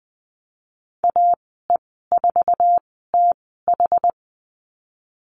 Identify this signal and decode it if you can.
Morse code, heard as a keyed tone, about 20 wpm, 700 Hz. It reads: A E 4 T H
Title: 076 Lesson 13g - Letter L - Call Signs 20wpm